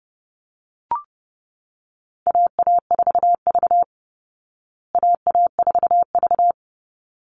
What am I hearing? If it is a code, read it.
Morse code, heard as a keyed tone, about 30 wpm, 700 Hz. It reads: AA4V AA4V
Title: Call Signs Repeated 3x - 30wpm